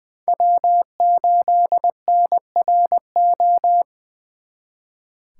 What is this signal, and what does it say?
Morse code, heard as a keyed tone, about 23 wpm, 700 Hz. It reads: W8NRO
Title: Speed Racing - USA 1x3 Call Signs 20wpm